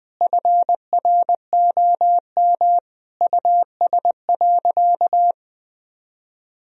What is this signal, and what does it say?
Morse code, heard as a keyed tone, about 21 wpm, 700 Hz. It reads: FROM US.
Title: Sentences from Top 1000 Words - Review Part 1 - 20wpm